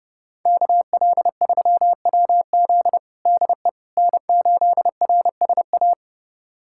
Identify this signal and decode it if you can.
Morse code, heard as a keyed tone, about 30 wpm, 700 Hz. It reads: KL3WZ DE N8RSA
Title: USA Callsign DE Callsign - 5 Characters - 30wpm